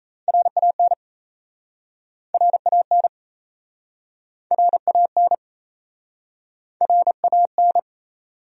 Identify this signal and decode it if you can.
Morse code, heard as a keyed tone, about 37 wpm, 700 Hz. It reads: RAN RAN RAN RAN